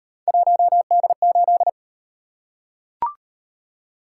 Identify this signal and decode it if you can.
Morse code, heard as a keyed tone, about 35 wpm, 700 Hz. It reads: 1D8